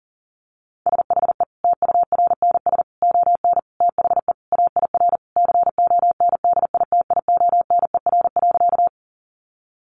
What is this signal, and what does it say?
Morse code, heard as a keyed tone, about 40 wpm, 700 Hz. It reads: SHE TURNS ON THE AIR CONDITIONER.
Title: Additional Easy Phrases 40wpm